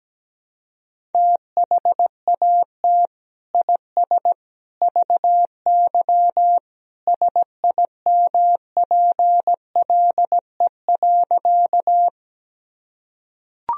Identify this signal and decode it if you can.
Morse code, heard as a keyed tone, about 17 wpm, 700 Hz. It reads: THAT IS VY SIMPLE.
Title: Sentences from Top 400 Words and Abbreviations 17wpm